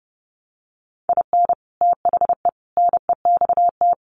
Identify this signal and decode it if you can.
Morse code, heard as a keyed tone, about 30 wpm, 700 Hz. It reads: IN THE NEXT
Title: Sets of 5 Words Encore2 30wpm